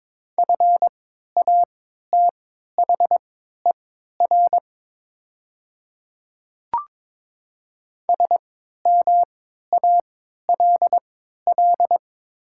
Morse code, heard as a keyed tone, about 22 wpm, spaced as for 12 wpm, 700 Hz. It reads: FATHER SMALL
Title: 3x Character Spacing - Top 500 Words - 22wpm